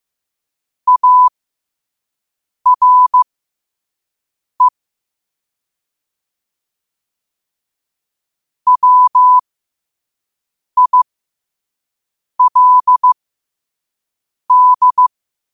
Morse code, heard as a keyed tone, about 15 wpm, spaced as for 5 wpm, 1000 Hz. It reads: ARE WILD
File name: qotdfiles / 5wpm